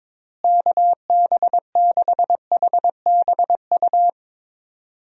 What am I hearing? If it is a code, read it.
Morse code, heard as a keyed tone, about 22 wpm, 700 Hz. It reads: KB6HBU
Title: USA Call Signs - 6 Characters - Repeated 2x - 22wpm